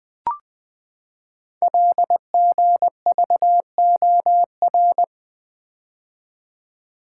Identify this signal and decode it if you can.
Morse code, heard as a keyed tone, about 20 wpm, 700 Hz. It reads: LGVOR
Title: Code Groups - 5 Letters - 20wpm